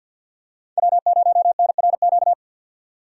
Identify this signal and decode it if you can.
Morse code, heard as a keyed tone, about 50 wpm, 700 Hz. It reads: W0NRQ
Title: USA Call Signs - 5 Characters - 4x Character Spacing - 50wpm